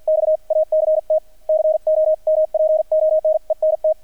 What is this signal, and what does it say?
Morse code, heard as a keyed tone, about 48 wpm, spaced as for 42 wpm, 620 Hz, with radio noise. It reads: XACT COMPONENT
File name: archived / CRSnet